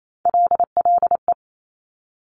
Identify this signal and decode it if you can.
Morse code, heard as a keyed tone, about 28 wpm, 700 Hz. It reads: LLE
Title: Top Three Letter Combinations in Words 28wpm